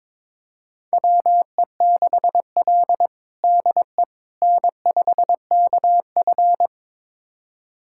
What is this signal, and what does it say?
Morse code, heard as a keyed tone, about 22 wpm, 700 Hz. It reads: WE6L DE N5KF
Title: USA Callsign DE Callsign - 4 Characters - 22wpm